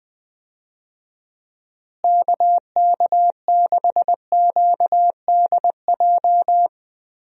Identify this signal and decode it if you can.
Morse code, heard as a keyed tone, about 21 wpm, 700 Hz. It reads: KK6QDJ
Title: USA Call Signs - 6 Characters - Repeated 3x - 20wpm